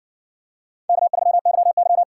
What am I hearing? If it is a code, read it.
Morse code, heard as a keyed tone, about 60 wpm, 700 Hz. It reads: K3YY